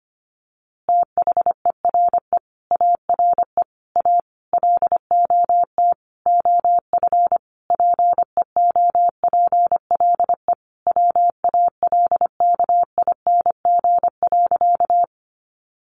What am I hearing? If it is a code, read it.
Morse code, heard as a keyed tone, about 25 wpm, 700 Hz. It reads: THERE ARE A LOT OF PEOPLE WALKING.
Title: Easy Phrases 25wpm